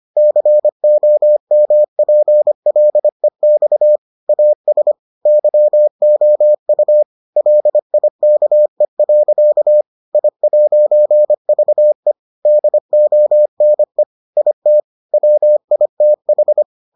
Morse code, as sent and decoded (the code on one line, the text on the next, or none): -.-. --- -- .--. .-.. -..- / .- ... / -.-- --- ..- / .-.. .. -.- . .-.-.- / .. .----. ...- . / -.. --- -. / .. - / .-- .. - ....
COMPLEX AS YOU LIKE. I'VE DONE IT WITH